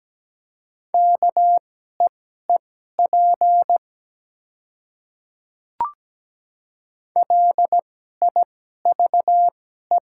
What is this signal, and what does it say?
Morse code, heard as a keyed tone, about 17 wpm, spaced as for 12 wpm, 700 Hz. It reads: KEEP LIVE
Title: 2x Character Spacing - Top 300 Words - 17wpm